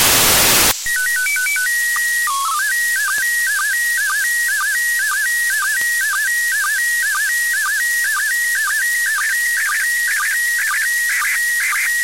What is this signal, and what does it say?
SSTV (slow-scan television), PD 120.
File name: ISS-SSTV-audio z 20-04-40 13-11-2024